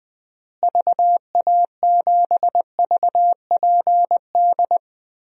Morse code, heard as a keyed tone, about 21 wpm, 700 Hz. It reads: VA7VPD
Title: DXCC-100 Call Signs - 6 Characters - Repeated 3x - 20wpm